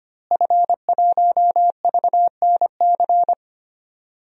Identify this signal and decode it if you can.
Morse code, heard as a keyed tone, about 25 wpm, 700 Hz. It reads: F1VNC
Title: DXCC-100 Call Signs - 5 Characters - Repeated 3x - 25wpm